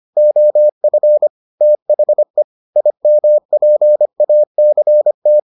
Morse code, heard as a keyed tone, about 25 wpm, 600 Hz. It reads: OF THE IMPACT